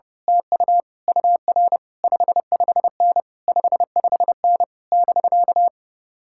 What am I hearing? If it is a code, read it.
Morse code, heard as a keyed tone, about 30 wpm, 700 Hz. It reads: TU UR 55N 55N <BK>
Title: Speed-Racing Common QSO Phrases Encore - 30wpm